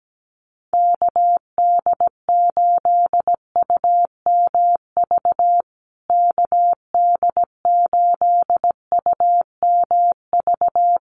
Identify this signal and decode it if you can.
Morse code, heard as a keyed tone, about 17 wpm, 700 Hz. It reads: KD8UMV KD8UMV
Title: USA Call Signs Repeated 2x 17wpm